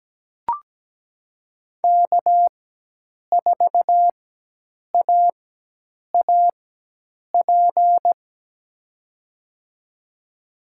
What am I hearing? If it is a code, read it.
Morse code, heard as a keyed tone, about 17 wpm, spaced as for 8 wpm, 700 Hz. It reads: K4AAP
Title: USA Call Signs - 5 Characters - 4x Character Spacing - 17wpm